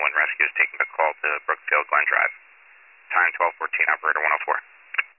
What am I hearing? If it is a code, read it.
Emergency Alert System (SAME) tones, heard as harsh buzzing bursts, with speech. It is heard from 0.5 to 4.2 s.